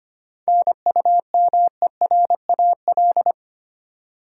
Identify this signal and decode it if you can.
Morse code, heard as a keyed tone, about 25 wpm, 700 Hz. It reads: NUMERAL
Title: Sentences from Top 1000 Words - Review Part 2 - New Word Drill - 25wpm